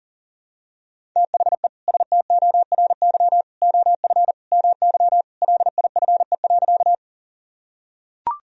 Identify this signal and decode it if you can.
Morse code, heard as a keyed tone, about 37 wpm, 700 Hz. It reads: THE STORY OF MY LIFE.
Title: Sentences from Top 1000 Words - Review Part 1 - 40wpm